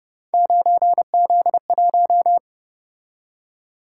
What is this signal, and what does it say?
Morse code, heard as a keyed tone, about 30 wpm, 700 Hz. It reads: 9Z1
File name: Speed-Racing - Three Letter-Number Combinations - 25wpm